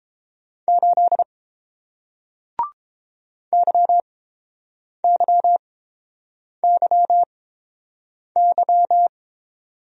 Morse code, heard as a keyed tone, about 30 wpm, spaced as for 7 wpm, 700 Hz. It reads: 8 YYYY